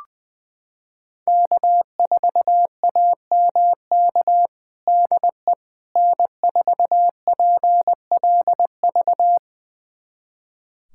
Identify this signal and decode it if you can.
Morse code, heard as a keyed tone, about 20 wpm, 700 Hz. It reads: K4AMK DE N4PLV
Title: USA Callsign DE Callsign - 5 Characters - 20wpm